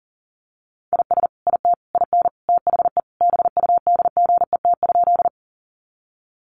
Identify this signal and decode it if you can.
Morse code, heard as a keyed tone, about 40 wpm, 700 Hz. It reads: IS IT IN THE BUDGET?
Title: Sentences from Top 1200 Words 40wpm